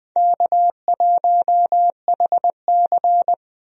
Morse code, heard as a keyed tone, about 20 wpm, 700 Hz. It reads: K1HC
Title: USA Call Signs - 4 Characters - Repeated 2x - 20wpm